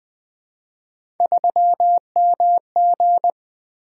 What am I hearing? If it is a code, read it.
Morse code, heard as a keyed tone, about 21 wpm, 700 Hz. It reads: 3MG
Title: Three Letter-Number Combinations - 20wpm